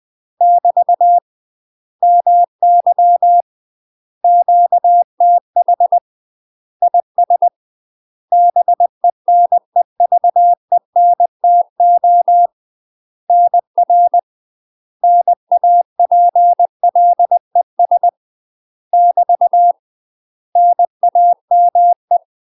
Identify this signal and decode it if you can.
Morse code, heard as a keyed tone, about 20 wpm, 700 Hz. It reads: <BT> MY QTH IS BENEVENTO NR NAPLES <BT> NAME
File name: UK MASTER QSOs TRAINING / 20WPM / UK MASTER QSOs 0022